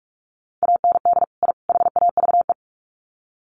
Morse code, heard as a keyed tone, about 44 wpm, 700 Hz. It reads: AND I HAVE
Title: Sets of 3 Words 45wpm